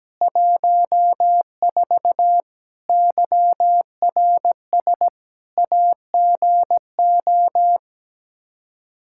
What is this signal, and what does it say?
Morse code, heard as a keyed tone, about 17 wpm, 700 Hz. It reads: 14 YRS AGO
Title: Practice QSO - Exchange 3 - More Info - 17wpm